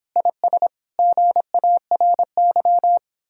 Morse code, heard as a keyed tone, about 31 wpm, 700 Hz. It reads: IS GARY